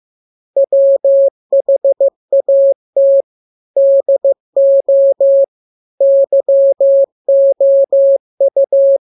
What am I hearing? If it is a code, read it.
Morse code, heard as a keyed tone, about 15 wpm, 550 Hz. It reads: WHAT DO YOU
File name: jokes / joke 12 15 0000